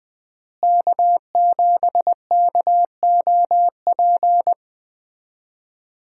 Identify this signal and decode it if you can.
Morse code, heard as a keyed tone, about 20 wpm, 700 Hz. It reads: K7KOP
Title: USA Call Signs - 5 Characters - Repeated 2x - 20wpm